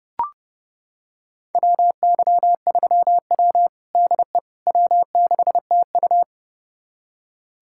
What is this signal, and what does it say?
Morse code, heard as a keyed tone, about 30 wpm, 700 Hz. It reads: WY3W DE W6TU
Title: USA Callsign DE Callsign - 4 Characters - 30wpm